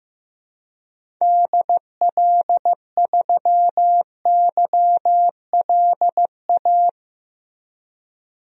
Morse code, heard as a keyed tone, about 15 wpm, 700 Hz. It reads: DL3YLA
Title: DXCC-100 Call Signs - 6 Characters - Repeated 2x - 15wpm